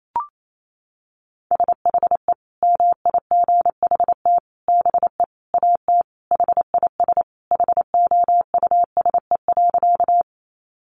Morse code, heard as a keyed tone, about 28 wpm, 700 Hz. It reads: SHE MIGHT BE AT HIS HOUSE.
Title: Sentences from Top 300 Words 28wpm